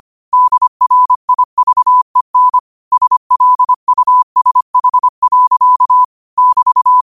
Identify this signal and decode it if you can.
Morse code, heard as a keyed tone, about 25 wpm, 1000 Hz. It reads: DRIVEN SLUSH. <BT>
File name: qotdfiles / 25wpm